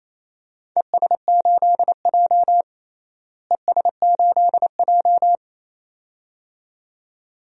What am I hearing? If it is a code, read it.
Morse code, heard as a keyed tone, about 27 wpm, 700 Hz. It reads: ES8J ES8J
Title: DXCC-100 Call Signs - 4 Characters - Repeated 2x - 28wpm